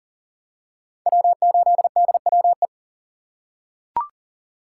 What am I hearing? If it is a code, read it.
Morse code, heard as a keyed tone, about 40 wpm, 700 Hz. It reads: W8DWE
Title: USA Call Signs - 5 Characters - 4x Character Spacing - 40wpm